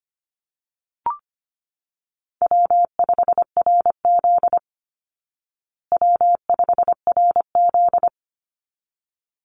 Morse code, heard as a keyed tone, about 25 wpm, 700 Hz. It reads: W5RZ W5RZ
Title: Call Signs Repeated 3x - 25wpm